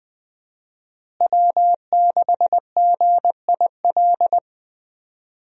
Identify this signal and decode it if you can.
Morse code, heard as a keyed tone, about 20 wpm, 700 Hz. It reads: W6GIL
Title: USA Call Signs - 5 Characters - Repeated 2x - 20wpm